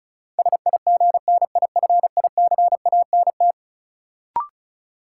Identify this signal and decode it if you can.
Morse code, heard as a keyed tone, about 36 wpm, 700 Hz. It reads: SIGNIFICANT
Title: Sentences from Top 1000 Words - Review Part 6 - New Word Drill - 35wpm